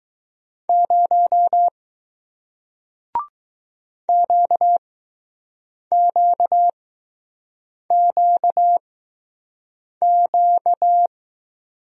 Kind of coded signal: Morse code

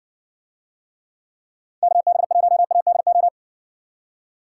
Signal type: Morse code

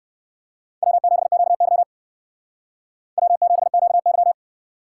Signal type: Morse code

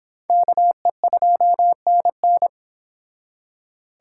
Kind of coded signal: Morse code

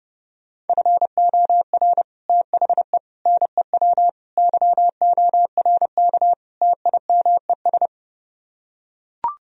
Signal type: Morse code

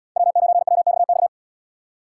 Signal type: Morse code

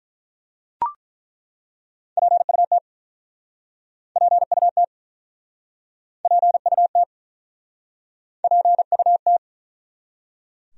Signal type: Morse code

